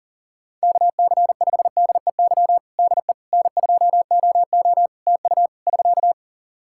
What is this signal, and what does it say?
Morse code, heard as a keyed tone, about 40 wpm, 700 Hz. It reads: KC5DEY DE N2OO TU <SK>
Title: K1USN SST QSO Practice 40wpm